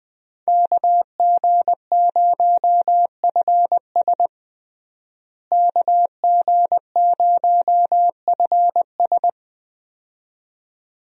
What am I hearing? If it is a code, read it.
Morse code, heard as a keyed tone, about 21 wpm, 700 Hz. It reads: KG0FS KG0FS
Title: USA Call Signs - 5 Characters - Repeated 2x - 20wpm